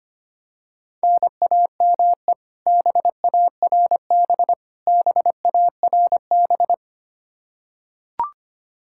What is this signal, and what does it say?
Morse code, heard as a keyed tone, about 25 wpm, 700 Hz. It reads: NAME BARB BARB
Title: Practice QSO - Exchange 1 - Beginning - 25wpm